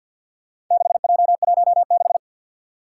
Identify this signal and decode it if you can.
Morse code, heard as a keyed tone, about 47 wpm, 700 Hz. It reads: 6J16
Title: Four Challenging Letter-Number Combinations 50wpm